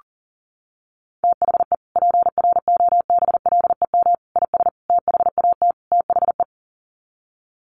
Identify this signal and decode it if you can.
Morse code, heard as a keyed tone, about 40 wpm, 700 Hz. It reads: THE PROBLEM IS THAT THE
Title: Sets of 5 Words Encore2 40wpm